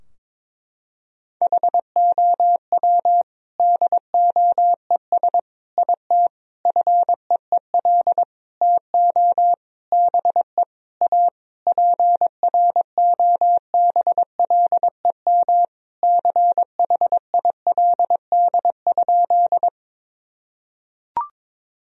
Morse code, heard as a keyed tone, about 22 wpm, 700 Hz. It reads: HOW DOES IT FEEL TO BE A PROBLEM CHILD?
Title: Sentences from Top 1000 Words - Review Part 2 - 22wpm